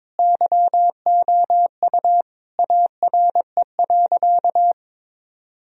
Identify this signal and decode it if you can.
Morse code, heard as a keyed tone, about 22 wpm, 700 Hz. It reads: YOU ARE.
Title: Sentences from Top 1000 Words - Review Part 4 - 22wpm